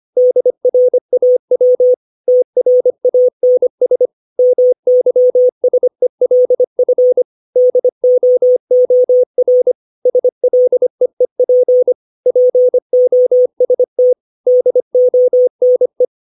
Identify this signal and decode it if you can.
Morse code, heard as a keyed tone, about 25 wpm, 500 Hz. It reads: DRAW TRANS MYSELF DOOR SLEEP POST DONE
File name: 632 words / 632 Words 25WPM